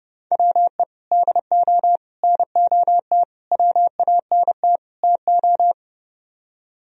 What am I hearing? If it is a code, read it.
Morse code, heard as a keyed tone, about 30 wpm, 700 Hz. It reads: WE DO NOT WANT TO